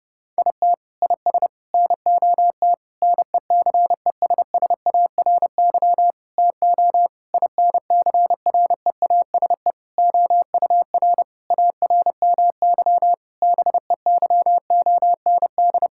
Morse code, heard as a keyed tone, about 30 wpm, 700 Hz. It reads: IT IS NOT NECESSARY TO INCREASE OUR ARMY BEYOND
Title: Sentences from Top 1000 Words - Review Part 6 - 30wpm